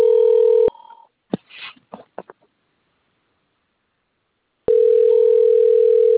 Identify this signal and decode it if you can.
Ringback tone (North America), 440 + 480 Hz.
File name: out- - - - - .82334